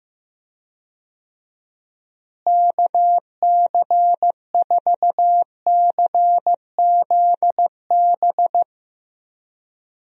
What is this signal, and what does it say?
Morse code, heard as a keyed tone, about 15 wpm, 700 Hz. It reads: KC4CZB
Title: USA Call Signs - 6 Characters - Repeated 2x - 15wpm